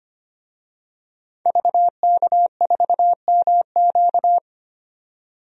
Morse code, heard as a keyed tone, about 25 wpm, 700 Hz. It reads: VK4MQ
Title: DXCC-100 Call Signs - 5 Characters - 25wpm